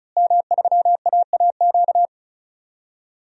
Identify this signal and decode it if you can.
Morse code, heard as a keyed tone, about 36 wpm, 700 Hz. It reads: M3AAQ
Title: DXCC-100 Call Signs - 5 Characters - Repeated 3x - 35wpm